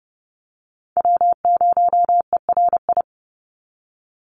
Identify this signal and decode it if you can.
Morse code, heard as a keyed tone, about 30 wpm, 700 Hz. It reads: W0ERI
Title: NA SOTA Chaser Callsigns 30wpm